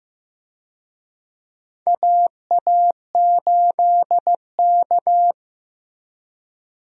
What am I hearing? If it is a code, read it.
Morse code, heard as a keyed tone, about 15 wpm, 700 Hz. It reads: AA8K
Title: USA Call Signs - 4 Characters - Repeated 3x - 15wpm